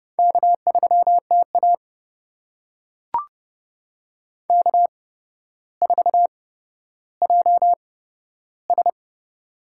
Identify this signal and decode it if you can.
Morse code, heard as a keyed tone, about 30 wpm, spaced as for 13 wpm, 700 Hz. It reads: K3TA K4JS
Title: USA Call Signs - 4 Characters - 8x Character Spacing - 30wpm